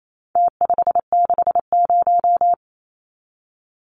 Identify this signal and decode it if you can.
Morse code, heard as a keyed tone, about 28 wpm, 700 Hz. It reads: T560
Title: Numbers - Contest Style - 4-Digit 28wpm